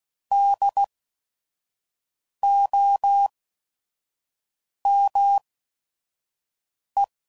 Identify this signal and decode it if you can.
Morse code, heard as a keyed tone, about 16 wpm, spaced as for 5 wpm, 800 Hz. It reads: DOME
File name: Prefixes / By First Letter / Ds